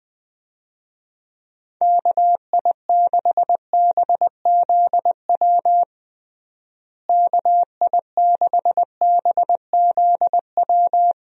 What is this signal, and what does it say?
Morse code, heard as a keyed tone, about 20 wpm, 700 Hz. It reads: KI6BZW KI6BZW
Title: USA Call Signs - 6 Characters - Repeated 3x - 20wpm